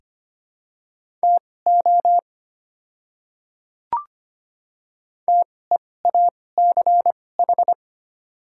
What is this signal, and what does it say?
Morse code, heard as a keyed tone, about 25 wpm, spaced as for 18 wpm, 700 Hz. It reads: TO TEACH
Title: 2x Character Spacing - Top 500 Words - 25wpm